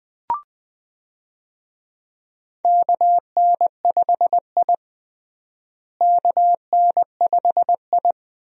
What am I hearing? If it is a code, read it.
Morse code, heard as a keyed tone, about 21 wpm, 700 Hz. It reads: KN5I KN5I
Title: USA Call Signs - 4 Characters - Repeated 2x - 20wpm